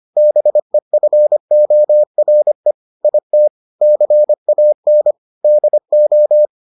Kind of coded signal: Morse code

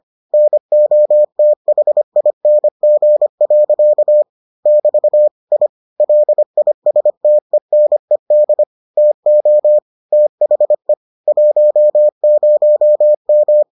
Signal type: Morse code

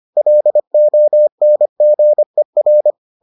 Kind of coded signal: Morse code

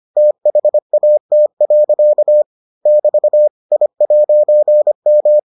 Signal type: Morse code

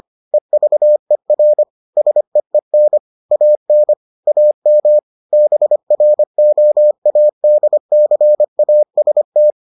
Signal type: Morse code

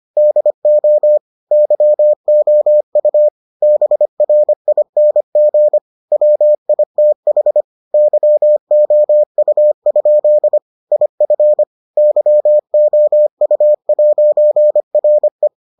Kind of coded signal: Morse code

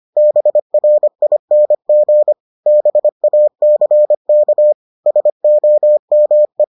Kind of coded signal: Morse code

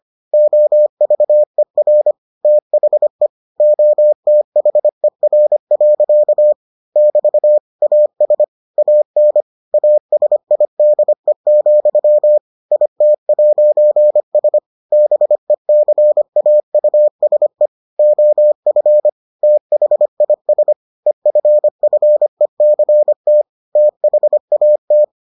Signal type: Morse code

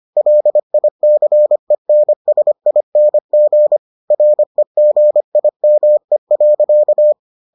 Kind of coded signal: Morse code

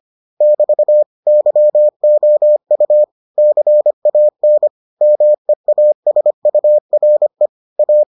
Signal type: Morse code

